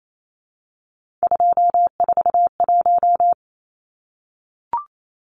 Morse code, heard as a keyed tone, about 28 wpm, 700 Hz. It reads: 241
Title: Numbers - 3-Digit 28wpm